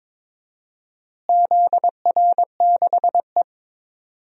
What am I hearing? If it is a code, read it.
Morse code, heard as a keyed tone, about 22 wpm, 700 Hz. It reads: ZR6E
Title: DXCC-100 Call Signs - 4 Characters - Repeated 3x - 22wpm